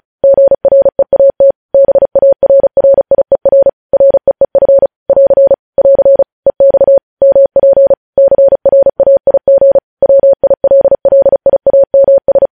Morse code, heard as a keyed tone, about 36 wpm, 550 Hz. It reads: GREAT BARRIER REEF <AR> <AR> EX MP CRAIG WILLIAMS
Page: news headlines in morse code at 35 wpm